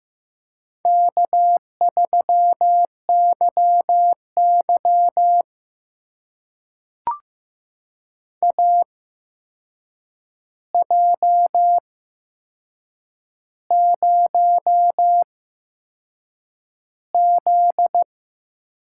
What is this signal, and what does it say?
Morse code, heard as a keyed tone, about 15 wpm, spaced as for 6 wpm, 700 Hz. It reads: K3YY AJ0Z
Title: USA Call Signs - 8x Character Spacing - 15wpm